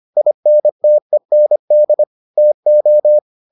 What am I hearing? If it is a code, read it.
Morse code, heard as a keyed tone, about 25 wpm, 600 Hz. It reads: INTEND TO